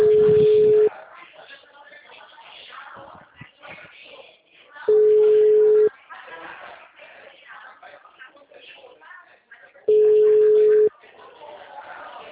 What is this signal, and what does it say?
Ringback tone (Europe, 425 Hz), with speech.